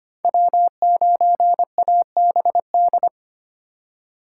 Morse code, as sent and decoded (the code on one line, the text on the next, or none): .-- ----. .- -... -..
W9ABD